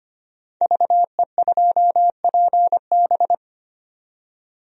Morse code, heard as a keyed tone, about 25 wpm, 700 Hz. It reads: VE2PB
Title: DXCC-100 Call Signs - 5 Characters - 25wpm